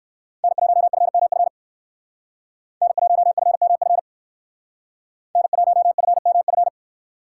Morse code, heard as a keyed tone, about 58 wpm, 700 Hz. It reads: N1FMF N1FMF N1FMF